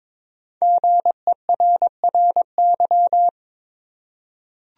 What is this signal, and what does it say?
Morse code, heard as a keyed tone, about 22 wpm, 700 Hz. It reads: GERRY
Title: USA Common Names - 22wpm